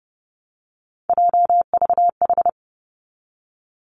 Morse code, heard as a keyed tone, about 30 wpm, 700 Hz. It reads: JVH
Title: Three Challenging Letter-Number Combinations 30wpm